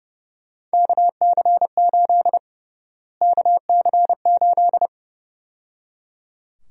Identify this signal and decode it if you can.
Morse code, heard as a keyed tone, about 30 wpm, 700 Hz. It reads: KC8 KC8
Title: USA Call Sign Prefixes - Repeated 2x - 30wpm